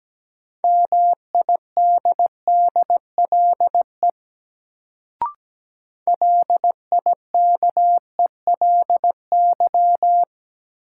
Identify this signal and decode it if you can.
Morse code, heard as a keyed tone, about 17 wpm, 700 Hz. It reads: MIDDLE LIKELY
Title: Sets of 2 Words - New Word Drill - 17wpm